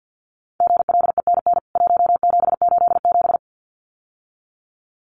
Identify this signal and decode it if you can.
Morse code, heard as a keyed tone, about 50 wpm, 700 Hz. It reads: GLENN 1787
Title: CWT Exchange 50wpm